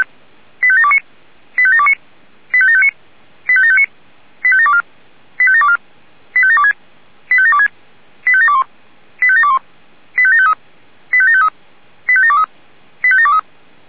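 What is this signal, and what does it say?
Five-tone selective call (ZVEI), heard as quick runs of five tones, tones of about 70 ms. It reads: 85729 85729 85759 85759 85724 85724 85726 85726 85721 85721 85753 85753 85723 85723
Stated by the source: ZVEI five-tone selective call received on an SDR.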